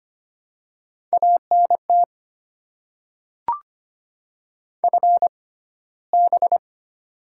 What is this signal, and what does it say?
Morse code, heard as a keyed tone, about 25 wpm, spaced as for 15 wpm, 690 Hz. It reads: ANT FB
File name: Top 20 QSO Elements - 6x Character Spacing - 25wpm